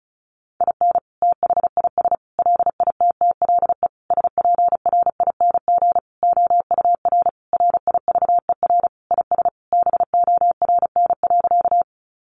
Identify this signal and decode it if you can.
Morse code, heard as a keyed tone, about 34 wpm, 700 Hz. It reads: IN THIS LITTLE SPRING OUR RIVER IS BORN.
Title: Sentences from Top 1200 Words 35wpm